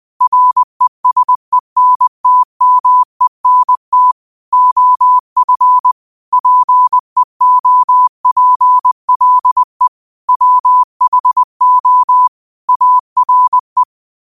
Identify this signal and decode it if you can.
Morse code, heard as a keyed tone, about 20 wpm, 1000 Hz. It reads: RESENTMENT OF PEOPLE WHO ARE
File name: qotdfiles / 20wpm